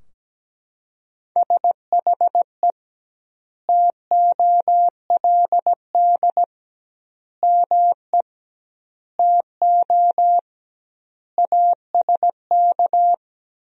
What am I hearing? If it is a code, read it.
Morse code, heard as a keyed tone, about 17 wpm, 700 Hz. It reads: SHE TOLD ME TO ASK
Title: 2x Word-Spacing - Sentences from Top 400 Words - 17wpm